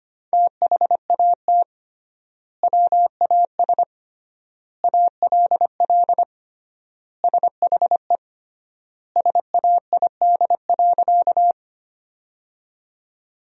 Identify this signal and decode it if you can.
Morse code, heard as a keyed tone, about 25 wpm, 700 Hz. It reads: THAT WAS ALL SHE SAID.
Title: 3x Word-Spacing - Sentences from Top 100 Words - 25wpm